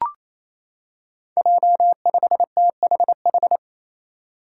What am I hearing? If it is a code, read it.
Morse code, heard as a keyed tone, about 28 wpm, 700 Hz. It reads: J5THH
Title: Five Challenging Letter-Number Combinations 28wpm